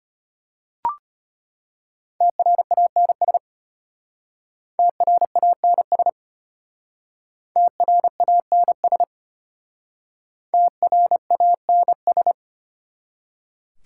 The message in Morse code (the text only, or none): TRANS TRANS TRANS TRANS